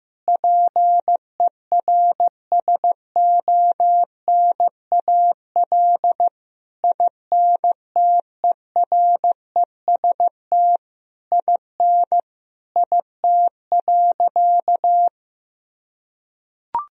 Morse code, heard as a keyed tone, about 15 wpm, 700 Hz. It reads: PERSONAL INTEREST IN IT.
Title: Sentences from Top 1000 Words - Review Part 4 - 15wpm